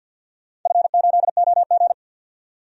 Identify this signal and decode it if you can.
Morse code, heard as a keyed tone, about 50 wpm, 700 Hz. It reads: F8OG
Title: DXCC-100 Call Signs - 4 Characters - 50wpm